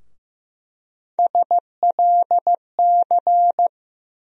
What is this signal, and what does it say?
Morse code, heard as a keyed tone, about 15 wpm, 700 Hz. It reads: SLC
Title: USA Call Sign Postfixes - 3 Letters 15wpm